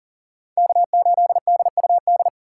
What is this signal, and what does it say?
Morse code, heard as a keyed tone, about 40 wpm, 700 Hz. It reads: K8DUD
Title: USA Call Signs - 5 Characters - 2x Character Spacing - 40wpm